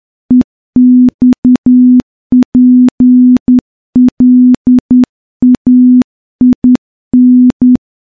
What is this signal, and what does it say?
Morse code, heard as a keyed tone, about 10 wpm, 250 Hz. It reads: EXPLAIN
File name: morse-training-10WPM